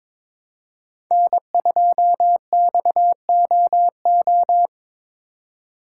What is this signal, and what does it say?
Morse code, heard as a keyed tone, about 22 wpm, 700 Hz. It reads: N2XOO
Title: USA Call Signs - 5 Characters - 4x Character Spacing - 22wpm